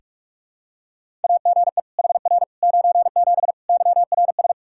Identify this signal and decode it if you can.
Morse code, heard as a keyed tone, about 46 wpm, 700 Hz. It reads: AGE HR 97 YRS
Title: Practice QSO - Exchange 3 - More Info - 45wpm